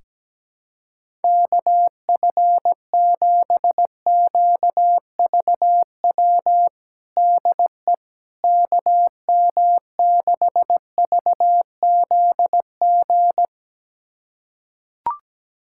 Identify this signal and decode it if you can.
Morse code, heard as a keyed tone, about 17 wpm, 700 Hz. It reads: KF7QVW DE KM6VZG
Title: USA Callsign DE Callsign 17wpm